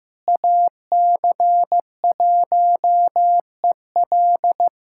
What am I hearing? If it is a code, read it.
Morse code, heard as a keyed tone, about 15 wpm, 700 Hz. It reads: AC1EL
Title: USA Callsign DE Callsign 15wpm